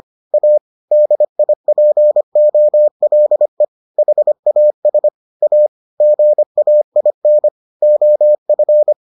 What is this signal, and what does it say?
Morse code, heard as a keyed tone, about 25 wpm, 600 Hz. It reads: A DIPOLE HAS A GAIN OF